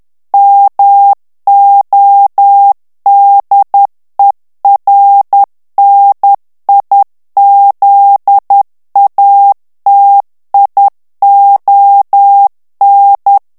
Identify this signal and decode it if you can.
Morse code, heard as a keyed tone, about 10 wpm, 800 Hz. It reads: MODERNIZATION
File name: ANGLAIS texte clair 3 vitesse 10 mots minute